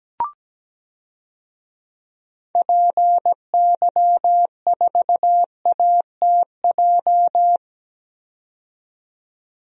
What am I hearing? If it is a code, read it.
Morse code, heard as a keyed tone, about 17 wpm, 700 Hz. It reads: PY4ATJ
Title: DXCC-100 Call Signs - 6 Characters - 17wpm